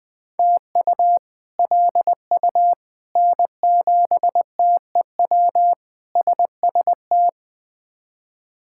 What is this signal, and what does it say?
Morse code, heard as a keyed tone, about 20 wpm, 700 Hz. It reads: TU LU N7TEW SST
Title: K1USN SST QSO Practice - 20wpm